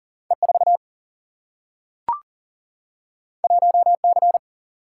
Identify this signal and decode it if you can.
Morse code, heard as a keyed tone, about 39 wpm, 700 Hz. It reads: E 4 1 C
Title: Single-Letter Single-Number 40wpm